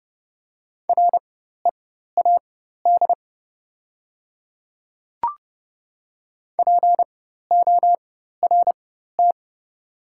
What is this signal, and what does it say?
Morse code, heard as a keyed tone, about 31 wpm, spaced as for 14 wpm, 690 Hz. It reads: READ PORT
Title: 4x Character Spacing - Top 200 Words - 30wpm